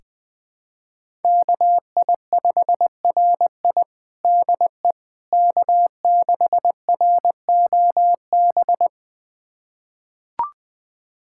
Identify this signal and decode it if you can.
Morse code, heard as a keyed tone, about 20 wpm, 700 Hz. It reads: KI5RI DE K6ROB
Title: USA Callsign DE Callsign - 5 Characters - 20wpm